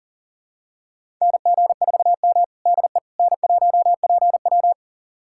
Morse code, heard as a keyed tone, about 40 wpm, 700 Hz. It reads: NG4M DE N1PW
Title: USA Callsign DE Callsign - 4 Characters - 40wpm